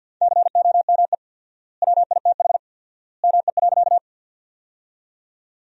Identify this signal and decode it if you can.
Morse code, heard as a keyed tone, about 52 wpm, 700 Hz. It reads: COME WITH ME.
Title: 4x Word-Spacing - Sentences from Top 200 Words - 50wpm